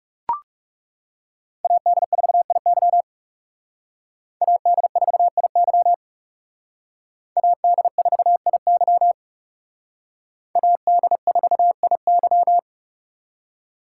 Morse code, heard as a keyed tone, about 34 wpm, 700 Hz. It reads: AD4IY AD4IY AD4IY AD4IY
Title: Speed Racing - USA 2x2 Call Signs 30wpm